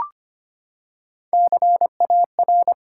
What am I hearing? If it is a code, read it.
Morse code, heard as a keyed tone, about 25 wpm, 700 Hz. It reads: CAR
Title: Three Letter Words - 25wpm